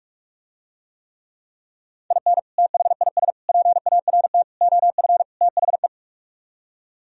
Morse code, heard as a keyed tone, about 49 wpm, 700 Hz. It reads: IN THIS PART OF THE